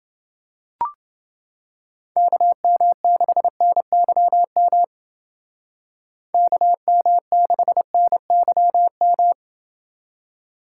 Morse code, heard as a keyed tone, about 28 wpm, 700 Hz. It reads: KM6NYM KM6NYM